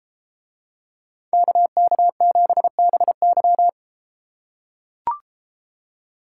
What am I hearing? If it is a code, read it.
Morse code, heard as a keyed tone, about 32 wpm, 700 Hz. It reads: KK7BY